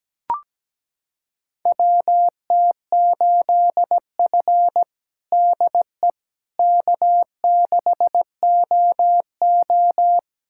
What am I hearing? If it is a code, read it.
Morse code, heard as a keyed tone, about 17 wpm, 700 Hz. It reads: WT8F DE K6OO
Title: USA Callsign DE Callsign - 4 Characters - 17wpm